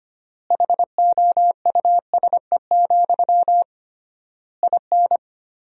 Morse code, heard as a keyed tone, about 25 wpm, 700 Hz. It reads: HOUSE, IN
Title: 3x Word-Spacing - Sentences from Top 300 Words - 25wpm